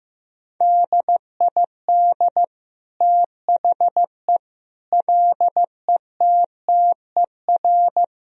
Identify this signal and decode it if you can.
Morse code, heard as a keyed tone, about 15 wpm, 700 Hz. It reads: DID THE LETTER
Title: Sentences from Top 1000 Words - Review Part 2 - 15wpm